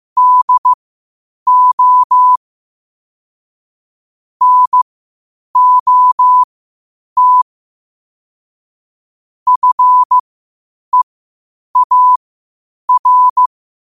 Morse code, heard as a keyed tone, about 15 wpm, spaced as for 8 wpm, 1000 Hz. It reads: DO NOT FEAR
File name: qotdfiles / 8wpm